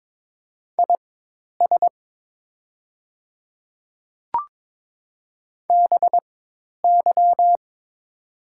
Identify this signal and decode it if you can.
Morse code, heard as a keyed tone, about 22 wpm, spaced as for 10 wpm, 700 Hz. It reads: IS BY
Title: 4x Character Spacing - Top 100 Words - 22wpm